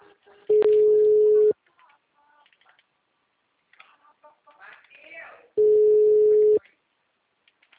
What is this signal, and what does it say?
Ringback tone (Europe, 425 Hz), with speech.